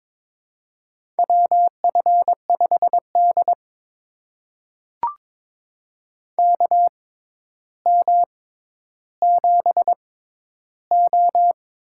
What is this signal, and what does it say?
Morse code, heard as a keyed tone, about 23 wpm, spaced as for 11 wpm, 700 Hz. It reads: WF5D KM7O
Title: USA Call Signs - 4 Characters - 6x Character Spacing - 22wpm